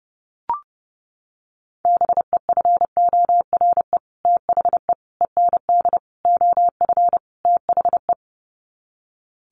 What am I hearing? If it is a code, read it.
Morse code, heard as a keyed tone, about 30 wpm, 700 Hz. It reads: BEFORE THE END OF THE
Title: Sets of 5 Words Encore 30wpm